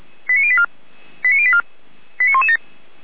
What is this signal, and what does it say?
Five-tone selective call (ZVEI), heard as quick runs of five tones, tones of about 70 ms. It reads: 89084 89084 89117